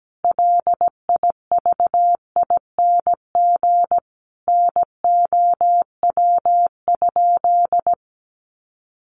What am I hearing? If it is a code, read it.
Morse code, heard as a keyed tone, about 17 wpm, 700 Hz. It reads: LIVING NOW?
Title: Sentences from Top 1100 Words 17wpm